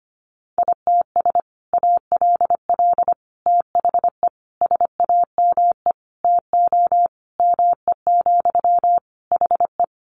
Morse code, heard as a keyed tone, about 25 wpm, 700 Hz. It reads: ITS ALL THE SAME TO ME, HE
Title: Sentences from Top 600 Words 25wpm